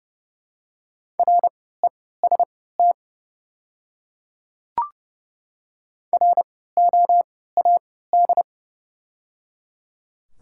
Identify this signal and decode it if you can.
Morse code, heard as a keyed tone, about 29 wpm, spaced as for 17 wpm, 700 Hz. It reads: REST ROAD
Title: 3x Character Spacing - Top 500 Words - 30wpm